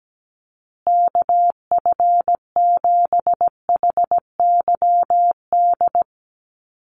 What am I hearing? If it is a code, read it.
Morse code, heard as a keyed tone, about 17 wpm, 700 Hz. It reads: KF7HYD
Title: USA Call Signs Repeated 2x 17wpm